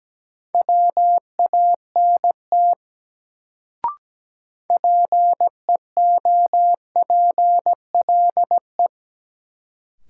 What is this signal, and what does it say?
Morse code, heard as a keyed tone, about 17 wpm, 700 Hz. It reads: WANT PEOPLE
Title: Sentences from Top 200 Words - New Word Drill - 17wpm